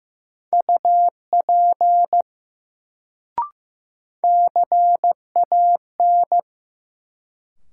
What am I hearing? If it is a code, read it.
Morse code, heard as a keyed tone, about 15 wpm, 700 Hz. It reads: UP CAN
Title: Sentences from Top 100 Words - New Word Drill - 15wpm